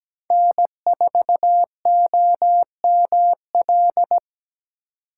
Morse code, heard as a keyed tone, about 17 wpm, 700 Hz. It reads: N4OML
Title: USA Call Signs - 5 Characters - 4x Character Spacing - 17wpm